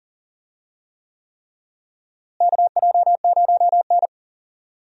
Morse code, heard as a keyed tone, about 37 wpm, 700 Hz. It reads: KJ0N
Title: USA Call Signs - 4 Characters - Repeated 3x - 40wpm